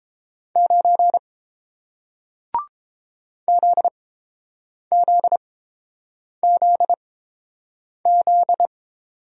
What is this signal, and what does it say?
Morse code, heard as a keyed tone, about 30 wpm, spaced as for 7 wpm, 700 Hz. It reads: 9 ZZZZ